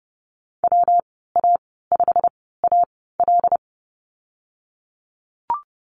Morse code, heard as a keyed tone, about 30 wpm, spaced as for 17 wpm, 700 Hz. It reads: WA5AL
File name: Lesson 19g - Letter w - Call Signs 30wpm